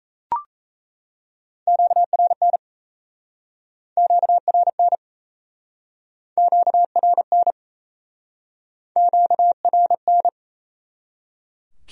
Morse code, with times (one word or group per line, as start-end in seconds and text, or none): 1.7-2.6 QRN
4.0-5.0 QRN
6.4-7.5 QRN
9.0-10.3 QRN